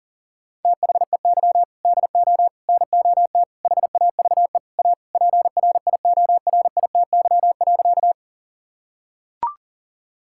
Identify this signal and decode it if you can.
Morse code, heard as a keyed tone, about 40 wpm, 700 Hz. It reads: THEY DO NOT HAVE A PRIORITY.
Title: Sentences from Top 1600 Words 40wpm